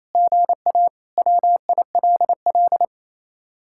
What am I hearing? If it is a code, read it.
Morse code, heard as a keyed tone, about 34 wpm, 700 Hz. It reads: GA WILL
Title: Speed-Racing Common QSO Phrases - 28wpm